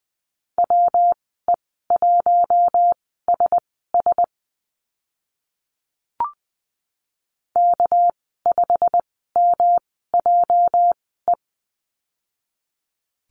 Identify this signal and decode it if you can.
Morse code, heard as a keyed tone, about 20 wpm, spaced as for 15 wpm, 700 Hz. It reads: WE1SS K5MJE
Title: 219 Lesson 35g - Number 0 - Call Signs 20wpm